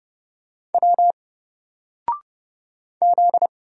Morse code, heard as a keyed tone, about 30 wpm, 700 Hz. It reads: W Z
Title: Single Letter-Number 30wpm